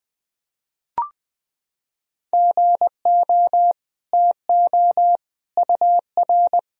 Morse code, heard as a keyed tone, about 20 wpm, 700 Hz. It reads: GO TO UR
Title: Sentences from Top 300 Words and Abbreviations 20wpm